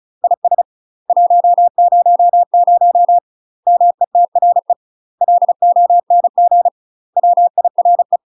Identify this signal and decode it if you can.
Morse code, heard as a keyed tone, about 34 wpm, 700 Hz. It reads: IS 100 METRE LONG WIRE
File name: UK MASTER QSOs TRAINING / 35WPM / UK MASTER QSOs 0037